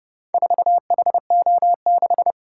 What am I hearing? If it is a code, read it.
Morse code, heard as a keyed tone, about 29 wpm, 700 Hz. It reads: 4HO6